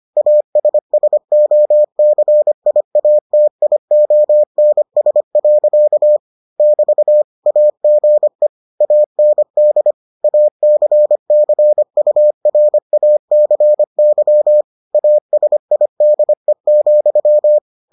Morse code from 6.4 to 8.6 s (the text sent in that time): <BT> AGE